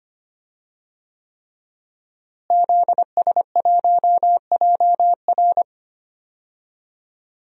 Morse code, heard as a keyed tone, about 25 wpm, 700 Hz. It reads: ZS1JR
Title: DXCC-100 Call Signs - 5 Characters - Repeated 3x - 25wpm